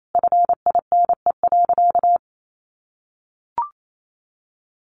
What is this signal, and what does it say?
Morse code, heard as a keyed tone, about 28 wpm, 700 Hz. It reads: FINE.
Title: Sentences from Top 1300 Words 28wpm